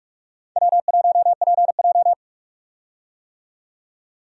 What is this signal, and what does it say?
Morse code, heard as a keyed tone, about 46 wpm, 700 Hz. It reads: W1PJ
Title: USA Call Signs - 4 Characters - 45wpm